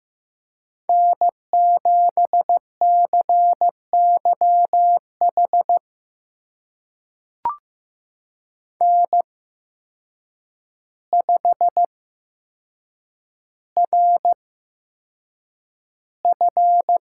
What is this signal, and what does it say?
Morse code, heard as a keyed tone, about 15 wpm, spaced as for 7 wpm, 700 Hz. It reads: N7CYH N5RF
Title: USA Call Signs - 5 Characters - 8x Character Spacing - 15wpm